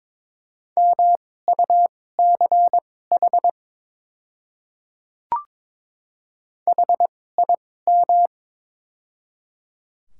Morse code, heard as a keyed tone, about 22 wpm, spaced as for 16 wpm, 700 Hz. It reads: MUCH HIM
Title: 2x Character Spacing - Top 200 Words - 22wpm